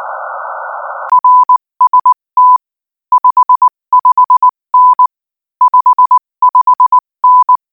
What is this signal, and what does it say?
Morse code, heard as a keyed tone, about 19 wpm, 1000 Hz. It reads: RST 55N 55N
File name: IZ4BEH-13-cm-AS